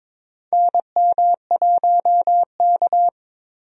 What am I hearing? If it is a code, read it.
Morse code, heard as a keyed tone, about 22 wpm, 700 Hz. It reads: NM1K
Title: USA Call Signs - 4 Characters - Repeated 3x - 22wpm